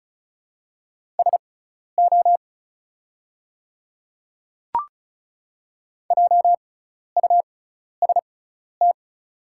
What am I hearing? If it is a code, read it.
Morse code, heard as a keyed tone, about 36 wpm, spaced as for 12 wpm, 690 Hz. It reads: SO JUST
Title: 6x Character Spacing - Top 200 Words - 35wpm